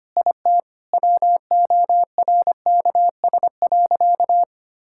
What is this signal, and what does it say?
Morse code, heard as a keyed tone, about 25 wpm, 700 Hz. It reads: IT WORKS.
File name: practice / 25wpm / Sentences with 4-Word Phrases - 25wpm